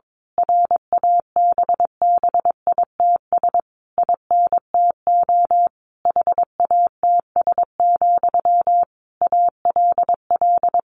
Morse code, heard as a keyed tone, about 22 wpm, 700 Hz. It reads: RABBITS INTO HATS, ALL
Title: A Sound Of Thunder 22wpm